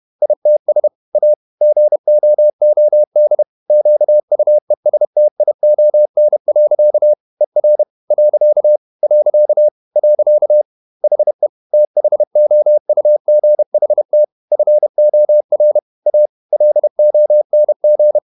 Morse code, heard as a keyed tone, about 31 wpm, 600 Hz. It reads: ITS A GOOD QUESTION. ER . . HE THOUGHT FOR A LONG
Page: Never 31wpm | CW med Gnister